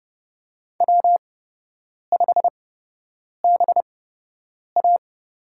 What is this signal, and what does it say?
Morse code, heard as a keyed tone, about 32 wpm, spaced as for 8 wpm, 700 Hz. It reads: W5BA
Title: USA Call Signs - 5 Characters - 8x Character Spacing - 30wpm